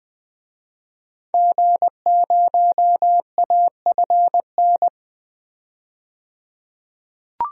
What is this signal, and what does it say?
Morse code, heard as a keyed tone, about 20 wpm, 700 Hz. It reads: G0AFN
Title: DXCC-100 Call Signs - 5 Characters - 20wpm